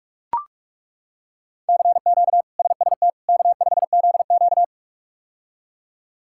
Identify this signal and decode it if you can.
Morse code, heard as a keyed tone, about 42 wpm, 700 Hz. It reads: CQ SST K5ZQ
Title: K1USN SST QSO Practice - 45wpm